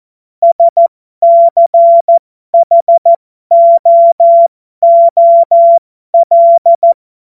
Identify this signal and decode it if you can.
Morse code, heard as a keyed tone, about 14 wpm, spaced as for 12 wpm, 680 Hz. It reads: SCHOOL
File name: Morsekurs / Witze